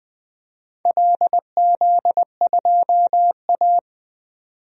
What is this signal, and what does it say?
Morse code, heard as a keyed tone, about 21 wpm, 700 Hz. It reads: LZ2A
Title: DXCC-100 Call Signs - 4 Characters - 20wpm